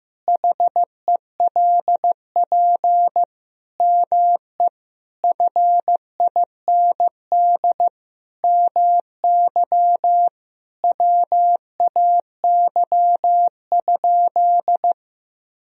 Morse code, heard as a keyed tone, about 15 wpm, 700 Hz. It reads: HELP ME FIND MY WAY?
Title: Sentences from Top 200 Words and Abbreviations 15wpm